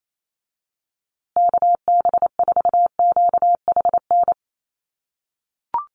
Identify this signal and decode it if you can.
Morse code, heard as a keyed tone, about 28 wpm, 700 Hz. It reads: KB4QHN
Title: USA Call Signs Repeated 3x 28wpm